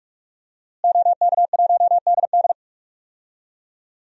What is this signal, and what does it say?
Morse code, heard as a keyed tone, about 46 wpm, 700 Hz. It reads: OK1DD
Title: DXCC-100 Call Signs - 5 Characters - Repeated 2x - 45wpm